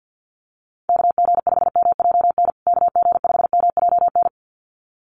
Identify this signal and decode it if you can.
Morse code, heard as a keyed tone, about 50 wpm, 700 Hz. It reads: KG5MJN KG5MJN
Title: USA Call Signs Repeated 2x 50wpm